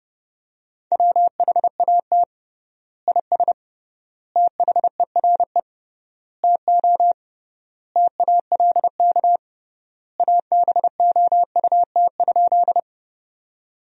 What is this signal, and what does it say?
Morse code, heard as a keyed tone, about 30 wpm, 700 Hz. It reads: WHAT IS THERE TO TALK ABOUT?
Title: 3x Word-Spacing - Sentences from Top 400 Words - 30wpm